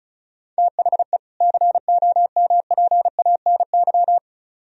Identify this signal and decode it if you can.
Morse code, heard as a keyed tone, about 39 wpm, 700 Hz. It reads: THE COMPANY